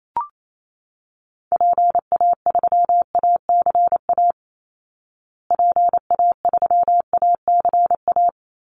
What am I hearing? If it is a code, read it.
Morse code, heard as a keyed tone, about 28 wpm, 700 Hz. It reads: PA3ACA PA3ACA
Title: Call Signs Repeated 3x - 28wpm